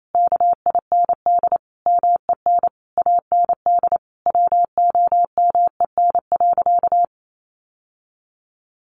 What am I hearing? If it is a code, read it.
Morse code, heard as a keyed tone, about 28 wpm, 700 Hz. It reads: KIND MEN AND WOMEN.
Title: Sentences from Top 1300 Words 28wpm